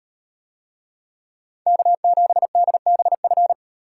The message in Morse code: K7DBF

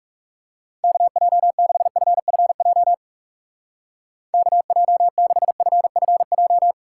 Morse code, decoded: KJ6FFJ KJ6FFJ

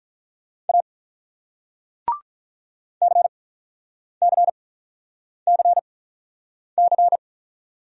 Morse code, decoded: A CCCC

